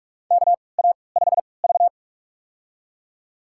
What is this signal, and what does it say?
Morse code, heard as a keyed tone, about 47 wpm, spaced as for 26 wpm, 700 Hz. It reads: KA5V
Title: USA Call Signs - 3x Character Spacing - 45wpm